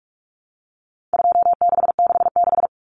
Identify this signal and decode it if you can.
Morse code, heard as a keyed tone, about 44 wpm, 700 Hz. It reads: 2666
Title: Common Cut Numbers - 4-Digit 45wpm